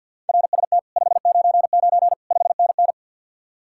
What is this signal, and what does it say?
Morse code, heard as a keyed tone, about 64 wpm, 700 Hz. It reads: RST 599 5NN